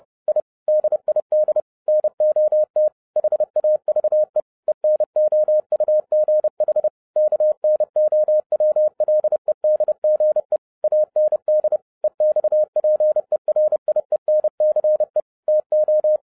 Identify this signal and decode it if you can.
Morse code, heard as a keyed tone, about 30 wpm, 610 Hz. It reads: I DID NOT HAVE ENOUGH KNOWLEDGE AND EXPERIENCE TO
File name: 2011-07-mp3 / n40 EXCITING QRP CW OPERATION 30wpm